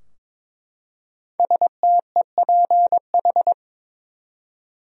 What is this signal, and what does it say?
Morse code, heard as a keyed tone, about 22 wpm, 700 Hz. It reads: STEPH
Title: USA Common Names - 22wpm